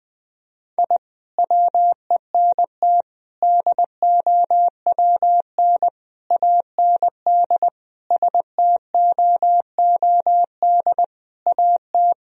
Morse code, heard as a keyed tone, about 21 wpm, 700 Hz. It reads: I WENT DOWN AND STOOD AT
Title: Sentences from Top 1000 Words - Review Part 2 - 20wpm